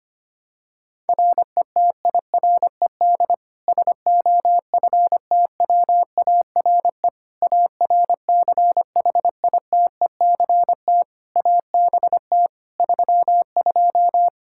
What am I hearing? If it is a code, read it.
Morse code, heard as a keyed tone, about 25 wpm, 700 Hz. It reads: RETIRED SOFTWARE ARCHITECT ABT 32
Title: Practice QSO - Exchange 3 - More Info - 25wpm